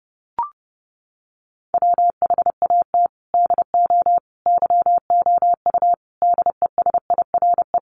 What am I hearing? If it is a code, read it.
Morse code, heard as a keyed tone, about 28 wpm, 700 Hz. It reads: WHAT DO YOU DESIRE
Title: Sentences from Top 1100 Words 30wpm